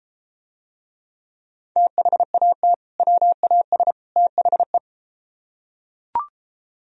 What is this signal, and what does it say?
Morse code, heard as a keyed tone, about 33 wpm, 700 Hz. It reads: THAT WAS THE